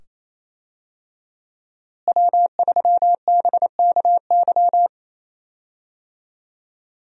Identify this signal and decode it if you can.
Morse code, heard as a keyed tone, about 28 wpm, 700 Hz. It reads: W3BKY
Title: USA Call Signs - 5 Characters - Repeated 2x - 28wpm